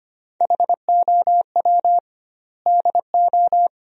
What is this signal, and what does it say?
Morse code, heard as a keyed tone, about 25 wpm, 700 Hz. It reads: HOW DO
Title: 2x Word-Spacing - Sentences from Top 400 Words - 25wpm